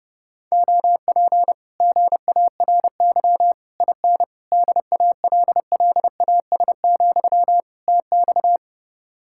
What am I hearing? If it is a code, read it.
Morse code, heard as a keyed tone, about 30 wpm, 700 Hz. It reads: OP GARY IN DALLAS, TX